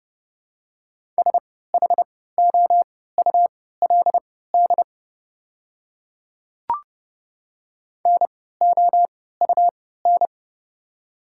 Morse code, heard as a keyed tone, about 30 wpm, spaced as for 17 wpm, 700 Hz. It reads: SHOULD NOUN
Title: 3x Character Spacing - Top 500 Words - 30wpm